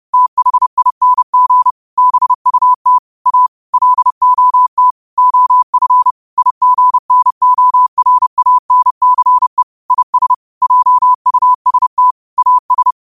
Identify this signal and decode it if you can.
Morse code, heard as a keyed tone, about 31 wpm, 1000 Hz. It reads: THING BUT A LOT OF IGNORANCE IS JUST AS
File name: qotdfiles / 30wpm